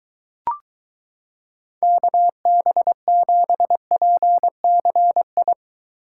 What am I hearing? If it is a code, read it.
Morse code, heard as a keyed tone, about 23 wpm, 700 Hz. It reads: KB7PCI